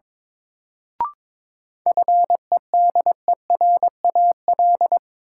Morse code, heard as a keyed tone, about 22 wpm, 700 Hz. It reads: FEDERAL
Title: Sentences from Top 1000 Words - Review Part 5 - New Word Drill - 22wpm